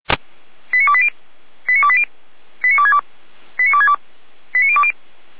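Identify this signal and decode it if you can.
Five-tone selective call (ZVEI), heard as quick runs of five tones, tones of about 70 ms. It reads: 89280 89280 89262 89262 89020
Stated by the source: ZVEI five-tone selective call 89050 received on SDR.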